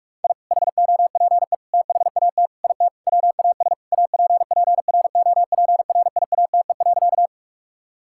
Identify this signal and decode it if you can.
Morse code, heard as a keyed tone, about 41 wpm, 700 Hz. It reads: I HOPE THAT IT WAS APPROPRIATE.
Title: Sentences from Top 1000 Words - Review Part 5 - 45wpm